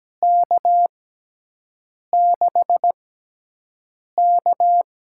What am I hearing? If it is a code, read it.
Morse code, heard as a keyed tone, about 17 wpm, spaced as for 6 wpm, 700 Hz. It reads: K6K
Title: USA Call Signs - 6 Characters - 6x Character Spacing - 17wpm